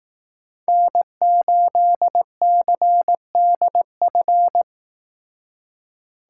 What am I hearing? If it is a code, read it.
Morse code, heard as a keyed tone, about 18 wpm, 700 Hz. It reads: N8CDF
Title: Speed Racing - USA 1x3 Call Signs 15wpm